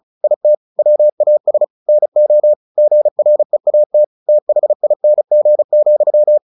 Morse code, heard as a keyed tone, about 34 wpm, 600 Hz. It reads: IT WAS NO GREAT THING,
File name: the old man and the sea 2 35wpm